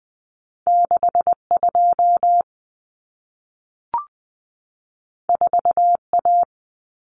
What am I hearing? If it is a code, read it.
Morse code, heard as a keyed tone, about 20 wpm, 700 Hz. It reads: 62 4A
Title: Common Cut Numbers - 2-Digit 20wpm